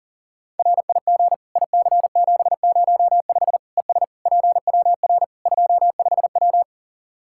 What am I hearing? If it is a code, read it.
Morse code, heard as a keyed tone, about 37 wpm, 700 Hz. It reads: RIG IC705 ES PWR 25W
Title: Practice QSO - Exchange 2 - Info - 40wpm